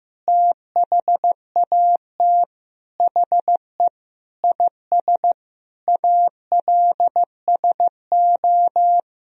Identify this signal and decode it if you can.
Morse code, heard as a keyed tone, about 15 wpm, 700 Hz. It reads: THAT HE IS ALSO
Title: Sentences from Top 1000 Words - Review Part 1 - 15wpm